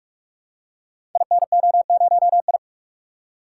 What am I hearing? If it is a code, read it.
Morse code, heard as a keyed tone, about 44 wpm, 700 Hz. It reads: INO0I
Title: Code Groups - 5 Letter:Numbers - 45wpm